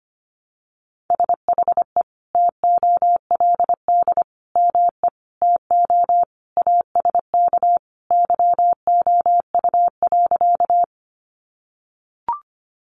Morse code, heard as a keyed tone, about 25 wpm, 700 Hz. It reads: SHE TOLD ME TO ASK YOU.
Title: Sentences from Top 400 Words 25wpm